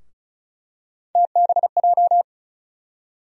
Morse code, heard as a keyed tone, about 36 wpm, 700 Hz. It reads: TBJ
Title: USA Call Sign Postfixes - 3 Letters 35wpm